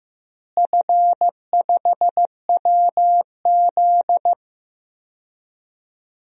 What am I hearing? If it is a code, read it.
Morse code, heard as a keyed tone, about 15 wpm, 700 Hz. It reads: F5WZ
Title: DXCC-100 Call Signs - 4 Characters - Repeated 3x - 15wpm